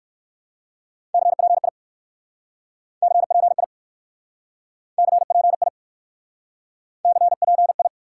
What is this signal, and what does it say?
Morse code, heard as a keyed tone, about 58 wpm, 700 Hz. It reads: CPI CPI CPI CPI